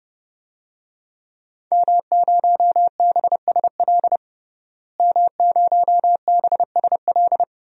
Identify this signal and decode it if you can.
Morse code, heard as a keyed tone, about 30 wpm, 700 Hz. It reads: M0BSL M0BSL
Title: DXCC-100 Call Signs - 5 Characters - Repeated 2x - 30wpm